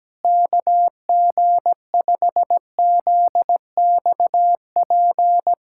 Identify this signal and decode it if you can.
Morse code, heard as a keyed tone, about 17 wpm, 700 Hz. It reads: KG5ZXP
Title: USA Callsign DE Callsign - 6 Characters - 17wpm